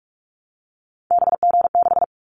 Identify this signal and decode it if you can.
Morse code, heard as a keyed tone, about 46 wpm, 700 Hz. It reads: BG6
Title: Callsign Prefixes for Top 100 DXCC - 45wpm